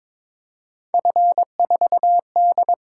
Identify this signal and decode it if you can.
Morse code, heard as a keyed tone, about 26 wpm, 700 Hz. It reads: F4D
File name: Speed-Racing - Three Letter-Number Combinations - 22wpm